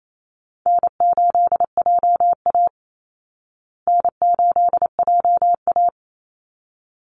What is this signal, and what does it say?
Morse code, heard as a keyed tone, about 28 wpm, 700 Hz. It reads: N8JA N8JA
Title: Easy Call Signs Repeated 2x - 28wpm